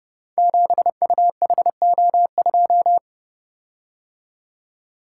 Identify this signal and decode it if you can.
Morse code, heard as a keyed tone, about 30 wpm, 700 Hz. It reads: 7UHO2
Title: Code Groups - 5 Letter:Numbers - 30wpm